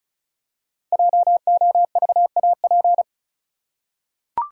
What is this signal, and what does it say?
Morse code, heard as a keyed tone, about 36 wpm, 700 Hz. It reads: JOVAP
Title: Code Groups - 5 Letters - 35wpm